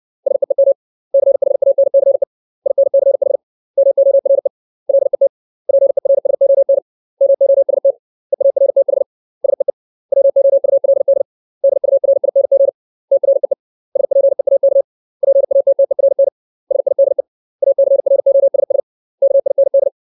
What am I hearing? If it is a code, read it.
Morse code, heard as a keyed tone, about 60 wpm, 550 Hz. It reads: SEEM CHANCE INCH MORE LET PERSON MOST EARTH SEE WOULD DURING TREE SPEAK PATTERN SIDE ACROSS KIND
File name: 632 words / 632 Words 60WPM